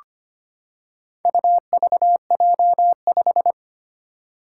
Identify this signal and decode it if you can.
Morse code, heard as a keyed tone, about 25 wpm, 700 Hz. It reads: UVJ5
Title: Four Challenging Letter-Number Combinations 25wpm